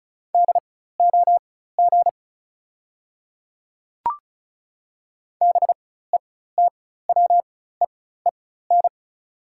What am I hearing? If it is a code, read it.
Morse code, heard as a keyed tone, about 34 wpm, spaced as for 16 wpm, 700 Hz. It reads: DOG BETWEEN
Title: 4x Character Spacing - Top 500 Words - 35wpm